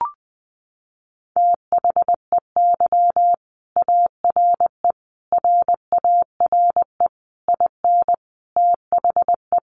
Morse code, heard as a keyed tone, about 19 wpm, 700 Hz. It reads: THEY ARE RARE IN THE
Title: Sentences from Top 1100 Words 20wpm